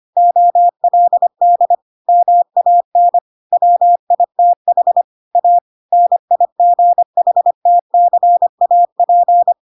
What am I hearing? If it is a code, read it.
Morse code, heard as a keyed tone, about 25 wpm, 700 Hz. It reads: OLD MAN WITH A NIGHTCAP
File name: Pinocchio / Pinocchio ch7 25 25 ik1hge